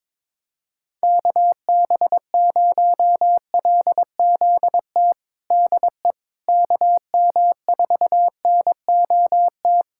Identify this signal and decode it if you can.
Morse code, heard as a keyed tone, about 22 wpm, 700 Hz. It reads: KB0LZT DE KM4NOT
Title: USA Callsign DE Callsign - 6 Characters - 22wpm